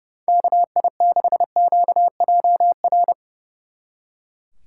Morse code, heard as a keyed tone, about 38 wpm, 700 Hz. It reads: KI6QJR